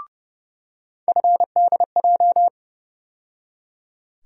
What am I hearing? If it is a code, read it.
Morse code, heard as a keyed tone, about 30 wpm, 700 Hz. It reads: FDJ
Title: USA Call Sign Postfixes - 3 Letters 30wpm